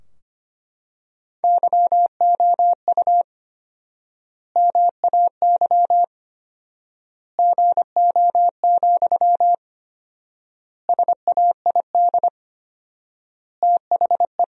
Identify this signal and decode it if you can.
Morse code, heard as a keyed tone, about 25 wpm, 700 Hz. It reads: YOU MAY GO, SAID THE
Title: 4x Word-Spacing - Sentences from Top 400 Words - 25wpm